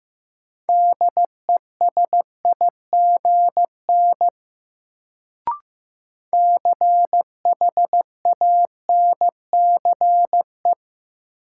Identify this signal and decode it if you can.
Morse code, heard as a keyed tone, about 15 wpm, 700 Hz. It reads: DESIGN CHANCE
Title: Sentences from Top 800 Words - New Word Drill - 15wpm